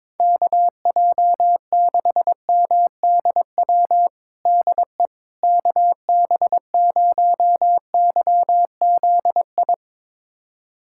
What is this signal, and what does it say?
Morse code, heard as a keyed tone, about 22 wpm, 700 Hz. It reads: KJ6MDW DE KB0YZI
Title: USA Callsign DE Callsign 22wpm